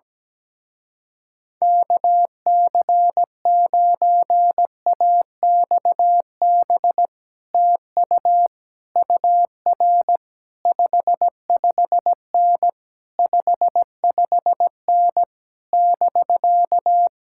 Morse code, heard as a keyed tone, about 17 wpm, 700 Hz. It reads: KC9AXB TU UR 55N 55N <BK>
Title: Speed-Racing Common QSO Phrases - 17wpm